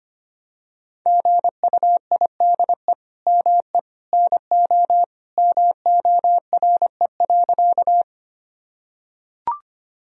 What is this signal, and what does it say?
Morse code, heard as a keyed tone, about 25 wpm, 700 Hz. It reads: GUIDE ME NO MORE.
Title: Sentences from Top 1000 Words - Review Part 3 - 25wpm